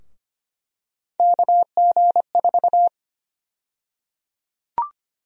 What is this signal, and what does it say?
Morse code, heard as a keyed tone, about 25 wpm, 700 Hz. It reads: KG4
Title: USA Call Sign Prefixes - 25wpm